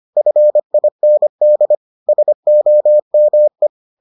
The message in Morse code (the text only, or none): FIND SOME